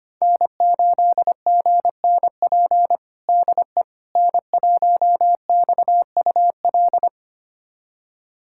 Morse code, heard as a keyed tone, about 25 wpm, 700 Hz. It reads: N8GNP DE N1XUL
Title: USA Callsign DE Callsign - 5 Characters - 25wpm